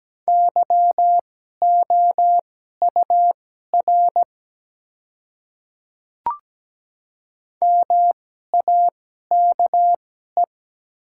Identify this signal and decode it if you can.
Morse code, heard as a keyed tone, about 17 wpm, spaced as for 12 wpm, 700 Hz. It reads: YOUR MAKE
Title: 2x Character Spacing - Top 100 Words - 17wpm